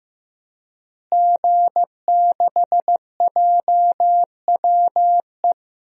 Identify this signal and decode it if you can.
Morse code, heard as a keyed tone, about 15 wpm, 700 Hz. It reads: G6JWE
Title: DXCC-100 Call Signs - 5 Characters - Repeated 3x - 15wpm